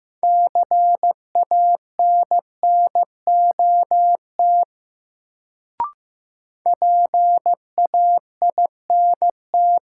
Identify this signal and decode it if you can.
Morse code, heard as a keyed tone, about 15 wpm, 700 Hz. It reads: CANNOT PAINT
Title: Sentences from Top 1000 Words - Review Part 6 - New Word Drill - 15wpm